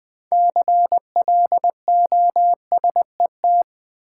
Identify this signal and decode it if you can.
Morse code, heard as a keyed tone, about 20 wpm, 700 Hz. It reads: CLOSET
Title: Six Letter Words - 20wpm